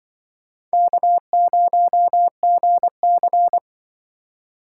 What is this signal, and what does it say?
Morse code, heard as a keyed tone, about 23 wpm, 700 Hz. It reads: K0GC
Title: Speed Racing - USA 1x2 Call Signs 20wpm